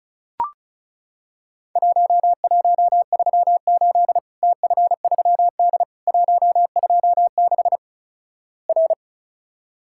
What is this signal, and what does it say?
Morse code, heard as a keyed tone, about 36 wpm, 700 Hz. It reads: 1138 TF3D 126 R
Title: Worked All Europe DX Contest - QTC Exchange - 35wpm